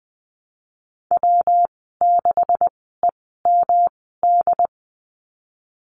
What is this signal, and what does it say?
Morse code, heard as a keyed tone, about 19 wpm, spaced as for 14 wpm, 700 Hz. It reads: W6EMD
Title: 161 Lesson 26g - Letter g - Call Signs 20wpm